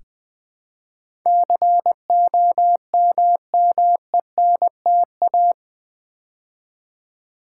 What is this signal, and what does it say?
Morse code, heard as a keyed tone, about 20 wpm, 700 Hz. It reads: COMMENTA
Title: Build A Word - 4 Letters At A Time - 11-Letter Words - 20wpm